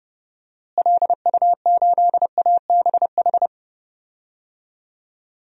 Morse code, heard as a keyed tone, about 30 wpm, 700 Hz. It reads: LU8ABH
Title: DXCC-100 Call Signs - 6 Characters - Repeated 2x - 30wpm